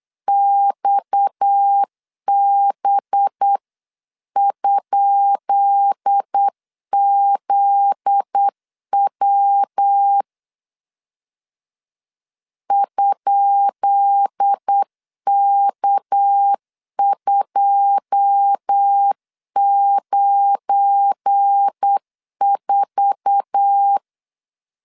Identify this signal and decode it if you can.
Morse code, heard as a keyed tone, about 8 wpm, 800 Hz. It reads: XB?ZW ?K294
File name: LEZIONI / 10 SIGNS / 10B INT 8wpm 3534